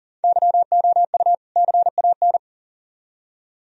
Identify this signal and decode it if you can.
Morse code, heard as a keyed tone, about 51 wpm, 690 Hz. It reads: YOU CAN